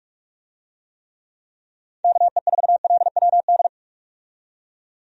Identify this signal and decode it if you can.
Morse code, heard as a keyed tone, about 46 wpm, 700 Hz. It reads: KE4LWD
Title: USA Call Signs - 6 Characters - 45wpm